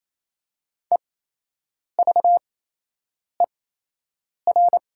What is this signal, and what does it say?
Morse code, heard as a keyed tone, about 28 wpm, spaced as for 8 wpm, 700 Hz. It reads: EVER
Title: 8x Character Spacing - Top 200 Words - 28wpm